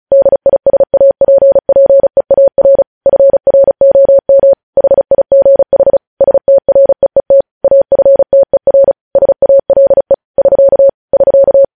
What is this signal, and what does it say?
Morse code, heard as a keyed tone, about 36 wpm, 550 Hz. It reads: DISAPPEAR FROM HIGH STREET AFTER SALE <SK> <SK>
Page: News headlines in Morse code at 35 wpm